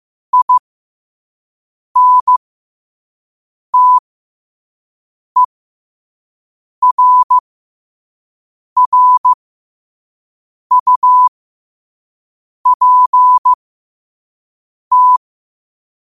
Morse code, heard as a keyed tone, about 15 wpm, spaced as for 5 wpm, 1000 Hz. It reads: INTERRUPT
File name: qotdfiles / 5wpm